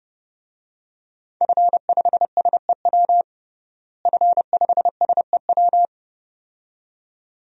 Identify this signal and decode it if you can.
Morse code, heard as a keyed tone, about 30 wpm, 700 Hz. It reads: F5SEW F5SEW
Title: DXCC-100 Call Signs - 5 Characters - Repeated 2x - 30wpm